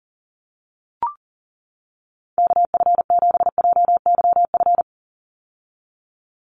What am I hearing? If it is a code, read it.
Morse code, heard as a keyed tone, about 40 wpm, 700 Hz. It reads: KF7JYF
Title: USA Call Signs 40wpm